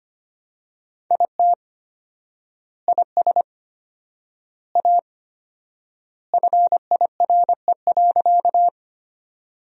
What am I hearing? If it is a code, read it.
Morse code, heard as a keyed tone, about 25 wpm, 700 Hz. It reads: IT IS A FIRE.
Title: 4x Word-Spacing - Sentences from Top 400 Words - 25wpm